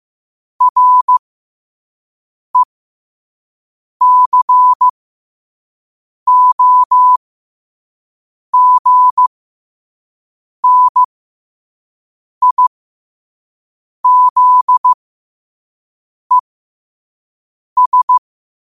Morse code, heard as a keyed tone, about 15 wpm, spaced as for 5 wpm, 1000 Hz. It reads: RECOGNIZES
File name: qotdfiles / 5wpm